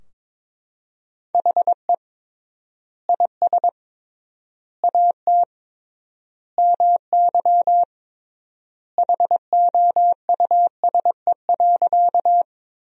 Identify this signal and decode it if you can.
Morse code, heard as a keyed tone, about 22 wpm, 700 Hz. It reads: HE IS AT MY HOUSE.
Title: 3x Word-Spacing - Sentences from Top 200 Words - 22wpm